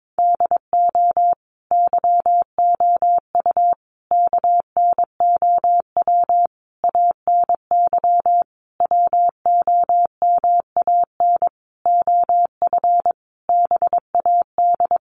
Morse code, heard as a keyed tone, about 22 wpm, 700 Hz. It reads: DO YOU KNOW ANY WOMAN OF BAD
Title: Sentences from Top 1200 Words 22wpm